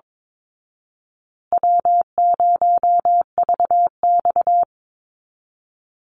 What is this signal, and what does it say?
Morse code, heard as a keyed tone, about 22 wpm, 700 Hz. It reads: W0VX
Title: Call Signs Repeated 2x - 22wpm